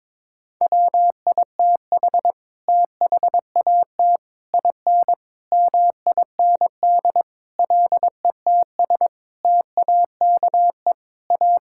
Morse code, heard as a keyed tone, about 22 wpm, 700 Hz. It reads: WITH THAT IN MIND LETS TAKE A
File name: Sets of 9 Words - 22wpm